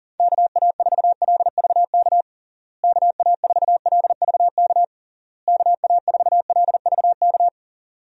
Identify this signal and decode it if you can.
Morse code, heard as a keyed tone, about 40 wpm, 700 Hz. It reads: KA4LVK KA4LVK KA4LVK
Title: USA Call Signs - 6 Characters - Repeated 3x - 40wpm